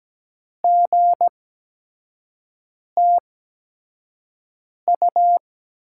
Morse code, heard as a keyed tone, about 17 wpm, spaced as for 5 wpm, 700 Hz. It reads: GTU
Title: USA Call Signs - 8x Character Spacing - 17wpm